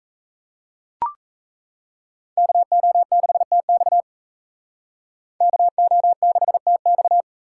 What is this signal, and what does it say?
Morse code, heard as a keyed tone, about 36 wpm, 700 Hz. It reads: KO6TX KO6TX